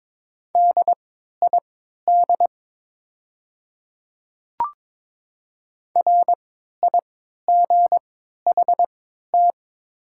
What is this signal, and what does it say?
Morse code, heard as a keyed tone, about 22 wpm, spaced as for 13 wpm, 700 Hz. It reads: DID RIGHT
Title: 3x Character Spacing - Top 300 Words - 22wpm